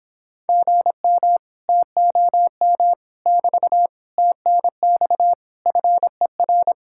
Morse code, heard as a keyed tone, about 26 wpm, 700 Hz. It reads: GM TOM <BT> TNX FER
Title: Speed-Racing Common QSO Phrases - 22wpm